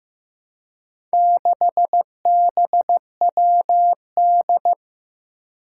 Morse code, heard as a keyed tone, about 15 wpm, 700 Hz. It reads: 6BWD
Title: Four Challenging Letter-Number Combinations 15wpm